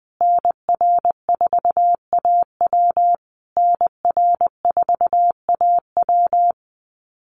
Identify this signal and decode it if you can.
Morse code, heard as a keyed tone, about 20 wpm, 700 Hz. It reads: NR4AW NR4AW
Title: USA Call Signs Repeated 3x 20wpm